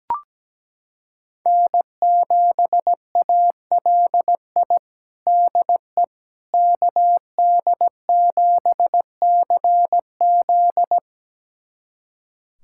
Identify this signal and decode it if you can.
Morse code, heard as a keyed tone, about 17 wpm, 700 Hz. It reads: N7ALI DE KD7CZ
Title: USA Callsign DE Callsign - 5 Characters - 17wpm